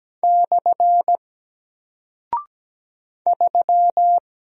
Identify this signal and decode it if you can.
Morse code, heard as a keyed tone, about 17 wpm, 700 Hz. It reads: / 3
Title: Single Character (Letter:Number:Punctuation:Prosign) Alt 17wpm